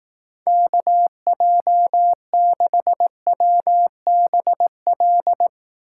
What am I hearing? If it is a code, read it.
Morse code, heard as a keyed tone, about 18 wpm, 700 Hz. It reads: KJ6WBL
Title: Speed Racing - USA Call Signs 15wpm